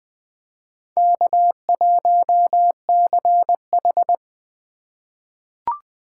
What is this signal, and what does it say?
Morse code, heard as a keyed tone, about 20 wpm, 700 Hz. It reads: K1CH
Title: USA Call Signs - 4x Character Spacing - 20wpm